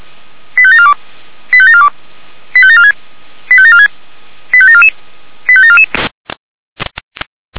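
Five-tone selective call (ZVEI), heard as quick runs of five tones, tones of about 70 ms. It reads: 85742 85742 85746 85746 85744 85744
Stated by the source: ZVEI five-tone selective call received by SDR.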